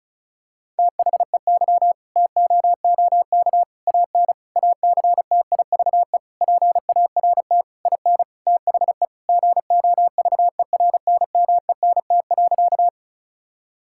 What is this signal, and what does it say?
Morse code, heard as a keyed tone, about 36 wpm, 700 Hz. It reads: THEY TOOK AN ACTIVE PART IN THE GOVERNMENT.
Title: Sentences from Top 1000 Words - Review Part 6 - 35wpm